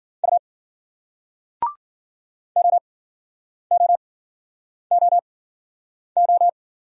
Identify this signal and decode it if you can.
Morse code, heard as a keyed tone, about 55 wpm, spaced as for 9 wpm, 700 Hz. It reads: U OOOO